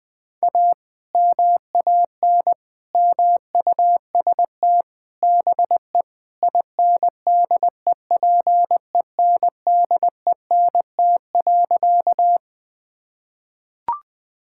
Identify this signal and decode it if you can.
Morse code, heard as a keyed tone, about 20 wpm, 700 Hz. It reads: A MAN MUST BE INDEPENDENT.
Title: Sentences from Top 1000 Words - Review Part 6 - 20wpm